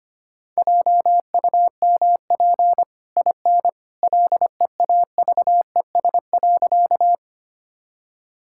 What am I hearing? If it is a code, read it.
Morse code, heard as a keyed tone, about 25 wpm, 700 Hz. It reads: JUMP IN LEAVES.
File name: Easy Sentences with Hints - 25wpm